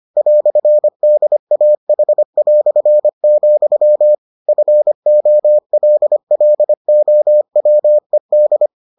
Morse code, heard as a keyed tone, about 25 wpm, 600 Hz. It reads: "DAH", FOLLOWED